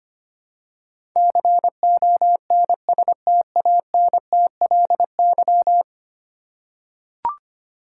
Morse code, heard as a keyed tone, about 25 wpm, 700 Hz. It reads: CONSTANTLY
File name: Ten Letter Words - 25wpm